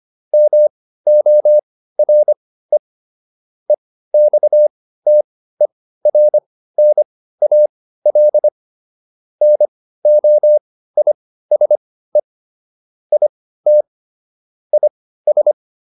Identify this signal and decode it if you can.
Morse code, heard as a keyed tone, about 25 wpm, spaced as for 15 wpm, 600 Hz. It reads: MORE EXTERNAL NOISE IT IS